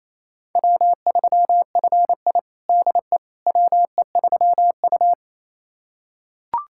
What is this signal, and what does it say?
Morse code, heard as a keyed tone, about 28 wpm, 700 Hz. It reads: W3FI DE WE3U
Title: USA Callsign DE Callsign - 4 Characters - 28wpm